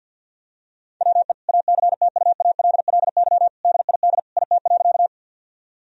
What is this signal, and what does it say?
Morse code, heard as a keyed tone, about 50 wpm, 700 Hz. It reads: WE ACTUALLY DID IT.
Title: Sentences from Top 1000 Words - Review Part 4 - 50wpm